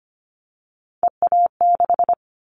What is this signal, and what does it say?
Morse code, heard as a keyed tone, about 25 wpm, 690 Hz. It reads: EA6
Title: Callsign Prefixes for Top 50 DXCC - 25wpm